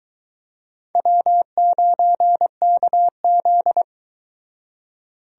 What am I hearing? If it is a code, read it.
Morse code, heard as a keyed tone, about 23 wpm, 700 Hz. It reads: W9KZ